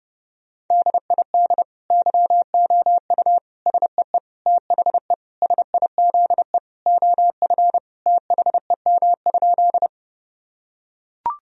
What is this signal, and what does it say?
Morse code, heard as a keyed tone, about 30 wpm, 700 Hz. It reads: DID YOU SEE THE SIZE OF THEM?
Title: Sentences from Top 1000 Words - Review Part 2 - 30wpm